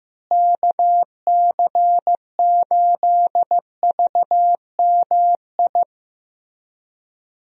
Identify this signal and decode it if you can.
Morse code, heard as a keyed tone, about 15 wpm, 700 Hz. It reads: KC8VMI
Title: USA Call Signs - 6 Characters - Repeated 3x - 15wpm